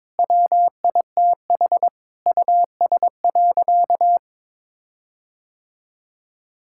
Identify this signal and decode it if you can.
Morse code, heard as a keyed tone, about 22 wpm, 700 Hz. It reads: WITH US.
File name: practice / 22wpm / Sentences with 8-Word Phrases - 22wpm